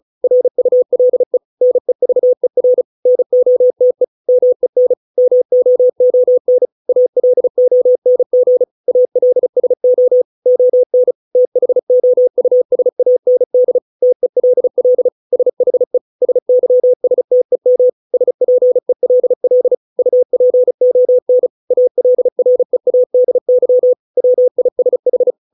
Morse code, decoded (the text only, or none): RULE NEVER NOTE MEN MOON ALONG ALSO ON THOUSAND TELL SHE SYSTEM SPELL UPON ALREADY WISH